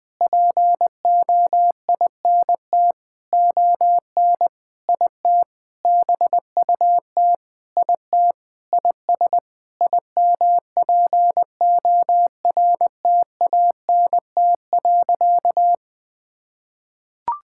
Morse code, heard as a keyed tone, about 20 wpm, 700 Hz. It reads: POINT ON IT BUT IT IS IMPORTANT.
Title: Sentences with 8-Word Phrases - Spoken Beginning - 20wpm